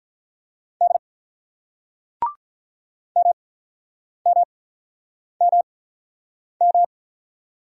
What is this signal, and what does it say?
Morse code, heard as a keyed tone, about 48 wpm, spaced as for 9 wpm, 690 Hz. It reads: D MMMM